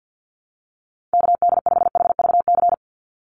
Morse code, heard as a keyed tone, about 50 wpm, 700 Hz. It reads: KD5HVC
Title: USA Call Signs 50wpm